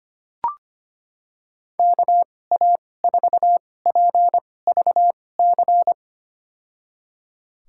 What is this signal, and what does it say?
Morse code, heard as a keyed tone, about 25 wpm, spaced as for 18 wpm, 700 Hz. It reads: KA4PVC
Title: USA Call Signs - 6 Characters - 2x Character Spacing - 25wpm